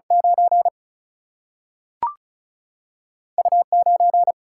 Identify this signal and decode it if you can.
Morse code, heard as a keyed tone, about 34 wpm, 700 Hz. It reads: 9 U9
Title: Single-Letter Single-Number 35wpm